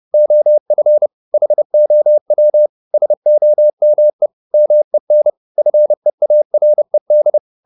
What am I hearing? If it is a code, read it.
Morse code, heard as a keyed tone, about 30 wpm, 600 Hz. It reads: OF HOW SOME MEN FEARED
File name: the old man and the sea / the old man and the sea 1 30wpm